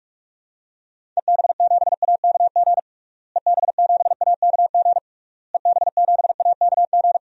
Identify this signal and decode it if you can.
Morse code, heard as a keyed tone, about 46 wpm, 700 Hz. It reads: EB7AKG EB7AKG EB7AKG
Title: DXCC-100 Call Signs - 6 Characters - Repeated 3x - 45wpm